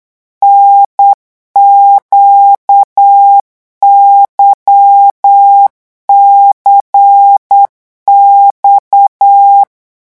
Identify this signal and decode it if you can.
Morse code, heard as a keyed tone, about 8 wpm, 800 Hz. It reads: NQYCX
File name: morse mp3 / revision3-vitesse 8 mots